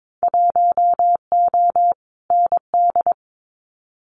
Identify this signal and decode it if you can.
Morse code, heard as a keyed tone, about 22 wpm, 700 Hz. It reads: 1O ND
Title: Winter Field Day Exchanges 22wpm